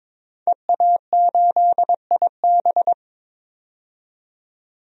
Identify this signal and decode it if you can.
Morse code, heard as a keyed tone, about 21 wpm, 700 Hz. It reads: EA8IB
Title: DXCC-100 Call Signs - 5 Characters - 22wpm